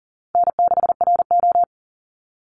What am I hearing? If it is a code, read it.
Morse code, heard as a keyed tone, about 40 wpm, 700 Hz. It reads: N6RO
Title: Easy Call Signs Repeated 3x - 40wpm